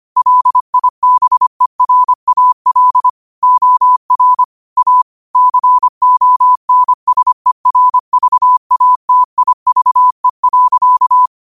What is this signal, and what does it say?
Morse code, heard as a keyed tone, about 25 wpm, 1000 Hz. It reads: LIBERAL OR A CONSERVATIVE.
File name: qotdfiles / 25wpm